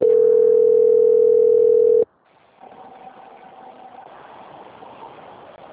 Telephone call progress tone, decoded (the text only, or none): ringback tone